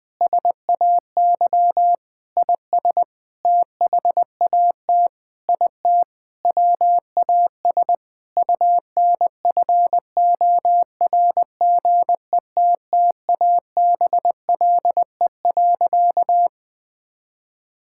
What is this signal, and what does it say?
Morse code, heard as a keyed tone, about 21 wpm, 700 Hz. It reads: SAY IS THAT IT WAS UNFORGETTABLE.
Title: Sentences with 6-Word Phrases - 20wpm